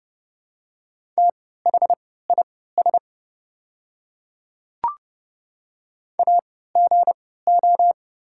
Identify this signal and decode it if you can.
Morse code, heard as a keyed tone, about 30 wpm, spaced as for 17 wpm, 700 Hz. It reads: THIS AGO
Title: 3x Character Spacing - Top 500 Words - 30wpm